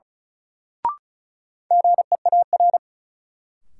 Morse code, heard as a keyed tone, about 36 wpm, 700 Hz. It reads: GEAR
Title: Sentences from Top 1400 Words - New Word Drill Alt - 35wpm